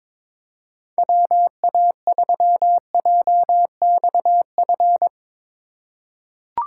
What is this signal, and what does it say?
Morse code, heard as a keyed tone, about 22 wpm, 700 Hz. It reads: WA3JXF
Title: USA Call Signs - 6 Characters - 3x Character Spacing - 22wpm